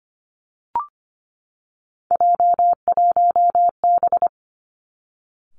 Morse code, heard as a keyed tone, about 25 wpm, 700 Hz. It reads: J1B
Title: Three Challenging Letter-Number Combinations 25wpm